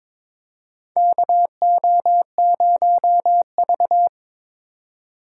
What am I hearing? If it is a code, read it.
Morse code, heard as a keyed tone, about 22 wpm, 700 Hz. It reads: KO0V
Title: USA Call Signs - 4 Characters - 4x Character Spacing - 22wpm